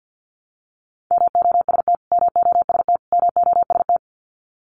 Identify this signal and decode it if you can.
Morse code, heard as a keyed tone, about 50 wpm, 700 Hz. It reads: MOST MOST MOST
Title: Top 500 Words Repeated 50wpm